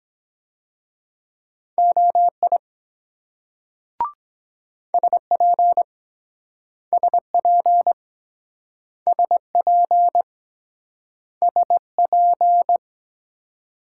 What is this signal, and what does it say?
Morse code, heard as a keyed tone, about 22 wpm, 700 Hz. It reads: OI SP SP SP SP